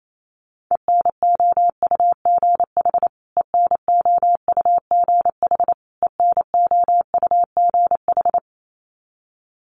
Morse code, heard as a keyed tone, about 28 wpm, 700 Hz. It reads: ENOUGH ENOUGH ENOUGH
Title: Top 500 Words Repeated 28wpm